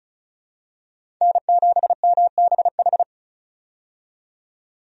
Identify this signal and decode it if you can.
Morse code, heard as a keyed tone, about 34 wpm, 700 Hz. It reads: N7MBH
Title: USA Call Signs - 5 Characters - 35wpm